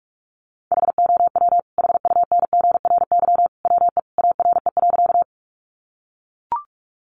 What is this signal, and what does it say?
Morse code, heard as a keyed tone, about 44 wpm, 700 Hz. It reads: HOW HUNGRY WE ARE.
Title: Sentences from Top 1400 Words 45wpm